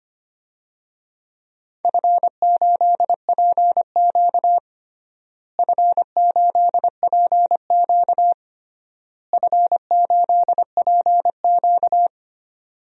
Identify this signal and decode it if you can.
Morse code, heard as a keyed tone, about 25 wpm, 700 Hz. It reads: F8PQ F8PQ F8PQ
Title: DXCC-100 Call Signs - 4 Characters - Repeated 3x - 25wpm